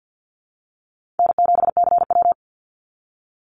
Morse code, heard as a keyed tone, about 55 wpm, 700 Hz. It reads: N7CW
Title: NA SOTA Activator Callsigns 50wpm